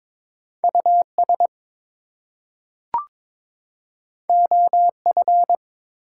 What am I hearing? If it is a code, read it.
Morse code, heard as a keyed tone, about 22 wpm, 700 Hz. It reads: US OF
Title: Two Letter Words - 22wpm